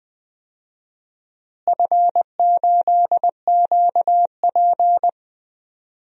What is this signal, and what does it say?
Morse code, heard as a keyed tone, about 20 wpm, 700 Hz. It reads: F8QP
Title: DXCC-100 Call Signs - 4 Characters - Repeated 3x - 20wpm